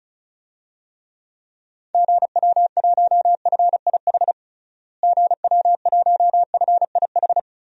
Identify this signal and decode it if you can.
Morse code, heard as a keyed tone, about 36 wpm, 700 Hz. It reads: GW1FIH GW1FIH
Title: DXCC-100 Call Signs - 6 Characters - Repeated 3x - 35wpm